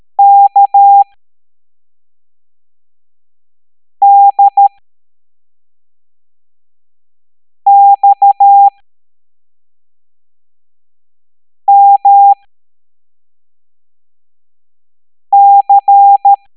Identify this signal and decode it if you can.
Morse code, heard as a keyed tone, about 13 wpm, spaced as for 3 wpm, 800 Hz. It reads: KDXMC